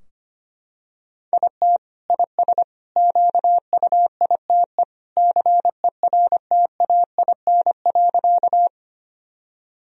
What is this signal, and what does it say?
Morse code, heard as a keyed tone, about 25 wpm, 700 Hz. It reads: IT IS QUITE CERTAIN.
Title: Sentences from Top 1000 Words - Review Part 2 - 25wpm